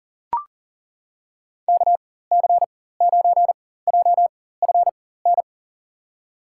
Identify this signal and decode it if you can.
Morse code, heard as a keyed tone, about 40 wpm, spaced as for 19 wpm, 700 Hz. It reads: KC9JFN
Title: USA Call Signs - 6 Characters - 4x Character Spacing - 40wpm